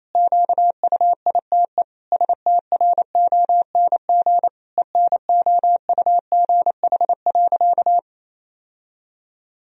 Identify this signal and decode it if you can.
Morse code, heard as a keyed tone, about 28 wpm, 700 Hz. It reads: QUITE STRONG ENOUGH.
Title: Sentences from Top 1000 Words - Review Part 3 - 28wpm